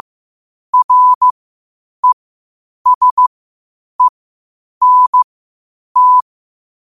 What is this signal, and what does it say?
Morse code, heard as a keyed tone, about 15 wpm, spaced as for 8 wpm, 1000 Hz. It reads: RESENT
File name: qotdfiles / 8wpm